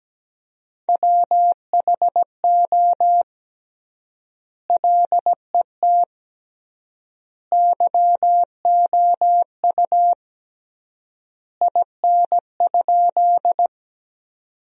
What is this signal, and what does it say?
Morse code, heard as a keyed tone, about 17 wpm, 700 Hz. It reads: WHO LET YOU IN?
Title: 3x Word-Spacing - Sentences from Top 300 Words - 17wpm